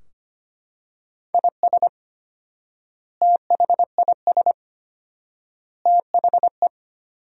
Morse code, heard as a keyed tone, about 25 wpm, 700 Hz. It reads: IS THIS THE
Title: 4x Word-Spacing - Sentences from Top 200 Words - 25wpm